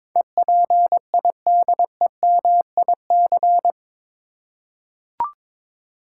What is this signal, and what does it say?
Morse code, heard as a keyed tone, about 22 wpm, 700 Hz. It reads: EPIDEMIC
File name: practice / 22wpm / Eight Letter Words - 22wpm